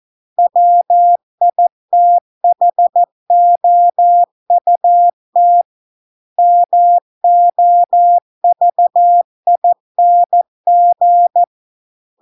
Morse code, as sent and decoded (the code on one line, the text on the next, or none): .-- .. - .... --- ..- - / -- --- ...- .. -. --.
WITHOUT MOVING